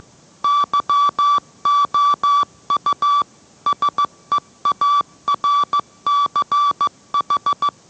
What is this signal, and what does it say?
Morse code, heard as a keyed tone, about 15 wpm, 1160 Hz. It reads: YOU SEARCH